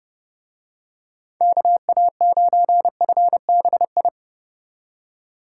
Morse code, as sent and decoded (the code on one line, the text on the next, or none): -.- .- ----. ..-. -... ..
KA9FBI